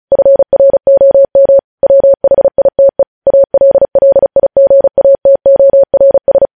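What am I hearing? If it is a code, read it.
Morse code, heard as a keyed tone, about 36 wpm, 550 Hz. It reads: FROM WHITE ALLIGATORS
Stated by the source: News headlines sent in Morse at 35 wpm.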